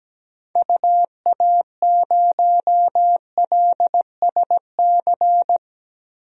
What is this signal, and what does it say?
Morse code, heard as a keyed tone, about 17 wpm, 700 Hz. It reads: UA0LSC
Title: DXCC-100 Call Signs - 6 Characters - Repeated 3x - 17wpm